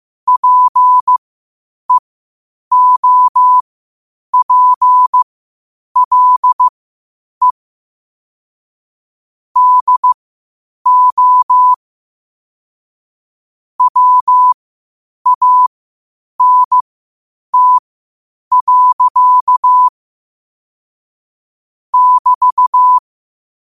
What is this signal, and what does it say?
Morse code, heard as a keyed tone, about 15 wpm, spaced as for 8 wpm, 1000 Hz. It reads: PEOPLE DO WANT. <BT>
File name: qotdfiles / 8wpm